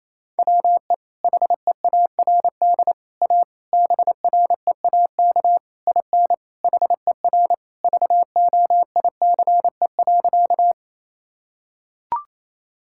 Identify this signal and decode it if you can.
Morse code, heard as a keyed tone, about 28 wpm, 700 Hz. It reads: WE HEARD A BREAK IN HER VOICE.
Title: Sentences from Top 1000 Words - Review Part 6 - 28wpm